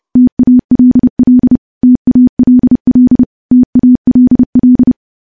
Morse code, heard as a keyed tone, about 30 wpm, 250 Hz. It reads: TALL TALL TALL
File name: morse / morse-training-30WPM